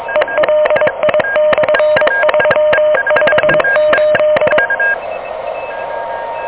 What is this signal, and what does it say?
Morse code, heard as a keyed tone, about 23 wpm, 600 Hz, with noise. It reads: ELI/358
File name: OK QRP 2023 / OK QRP 2023